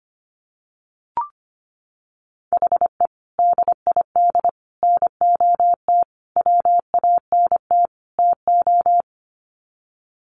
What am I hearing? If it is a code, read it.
Morse code, heard as a keyed tone, about 25 wpm, 700 Hz. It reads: HE DID NOT WANT TO
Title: Sets of 5 Words 25wpm